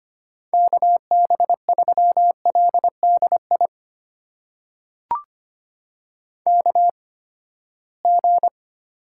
Morse code, heard as a keyed tone, about 25 wpm, spaced as for 17 wpm, 700 Hz. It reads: KB3LDI KG
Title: USA Call Signs - 6 Characters - 8x Character Spacing - 25wpm